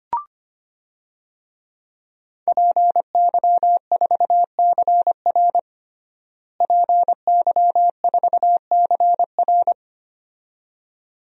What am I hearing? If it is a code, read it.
Morse code, heard as a keyed tone, about 25 wpm, 700 Hz. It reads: PY4CR PY4CR
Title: DXCC-100 Call Signs - 5 Characters - Repeated 2x - 25wpm